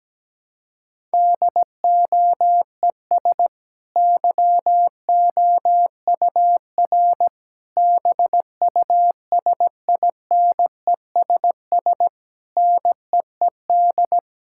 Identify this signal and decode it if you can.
Morse code, heard as a keyed tone, about 17 wpm, 700 Hz. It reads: DOES YOUR BUSINESS NEED
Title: Sentences from Top 1000 Words - Review Part 6 - 17wpm